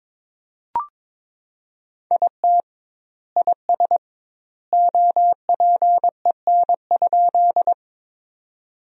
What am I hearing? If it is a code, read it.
Morse code, heard as a keyed tone, about 22 wpm, 700 Hz. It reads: IT IS OPEN?
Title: 2x Word-Spacing - Sentences from Top 300 Words - 22wpm